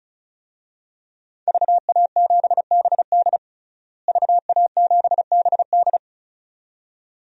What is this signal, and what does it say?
Morse code, heard as a keyed tone, about 34 wpm, 700 Hz. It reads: VA7BD VA7BD
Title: DXCC-100 Call Signs - 5 Characters - Repeated 2x - 35wpm